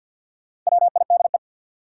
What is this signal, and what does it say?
Morse code, heard as a keyed tone, about 53 wpm, 700 Hz. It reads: WIDE
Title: Sentences from Top 900 Words - New Word Drill - 50wpm